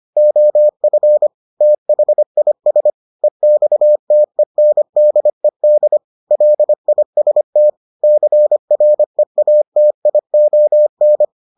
Morse code, heard as a keyed tone, about 25 wpm, 600 Hz. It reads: OF THIS EXTENDED LIST CREATION